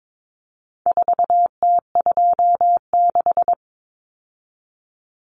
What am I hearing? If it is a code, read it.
Morse code, heard as a keyed tone, about 22 wpm, 700 Hz. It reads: 4T26
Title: Common Cut Numbers - 4-Digit 22wpm